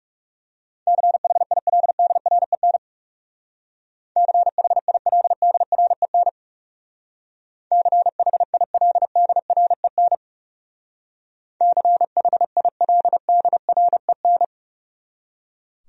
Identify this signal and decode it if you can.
Morse code, heard as a keyed tone, about 37 wpm, 700 Hz. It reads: CHILDREN CHILDREN CHILDREN CHILDREN
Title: Speed-Racing Top 500 Words 30wpm